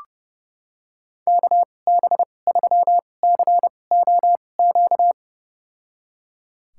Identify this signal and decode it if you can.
Morse code, heard as a keyed tone, about 29 wpm, spaced as for 21 wpm, 700 Hz. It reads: KB3COQ
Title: USA Call Signs - 6 Characters - 2x Character Spacing - 30wpm